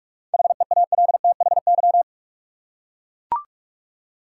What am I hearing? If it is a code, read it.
Morse code, heard as a keyed tone, about 42 wpm, 700 Hz. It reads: HEALTHY
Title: Seven Letter Words - 45wpm